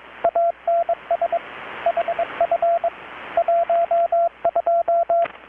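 Morse code, heard as a keyed tone, about 21 wpm, 680 Hz, with radio noise. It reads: ANS HF 12
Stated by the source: Morse from KPH's Night of Nights XX maritime radiotelegraphy event, received in Chiapas, Mexico about 3,800 km away on a Kenwood R-600 with a grounded 10 m long-wire antenna.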